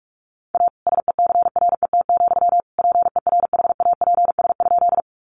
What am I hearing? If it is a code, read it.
Morse code, heard as a keyed tone, about 44 wpm, 700 Hz. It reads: A SECRET, PERHAPS?
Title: Sentences from Top 1200 Words 45wpm